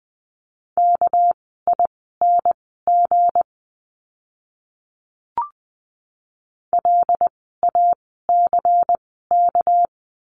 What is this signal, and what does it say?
Morse code, heard as a keyed tone, about 20 wpm, spaced as for 15 wpm, 700 Hz. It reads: KING LACK
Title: 199 Lesson 32f - Letter k - Words 20wpm